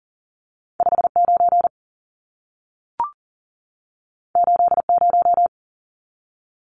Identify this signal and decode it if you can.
Morse code, heard as a keyed tone, about 40 wpm, 700 Hz. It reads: 59 80
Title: Numbers - 2-Digit 40wpm